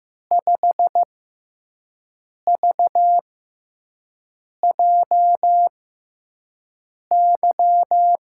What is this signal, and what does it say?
Morse code, heard as a keyed tone, about 15 wpm, spaced as for 5 wpm, 700 Hz. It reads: 5VJY
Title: USA Call Signs - 5 Characters - 6x Character Spacing - 15wpm